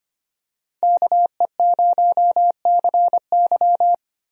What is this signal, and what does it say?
Morse code, heard as a keyed tone, about 25 wpm, 700 Hz. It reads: KE0CY
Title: USA Call Signs - 5 Characters - Repeated 2x - 25wpm